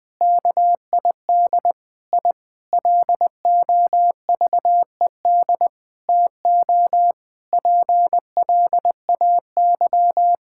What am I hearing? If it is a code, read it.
Morse code, heard as a keyed tone, about 20 wpm, 700 Hz. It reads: KID I LOVED TO PLAY
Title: Sentences with 5-Word Phrases - 20wpm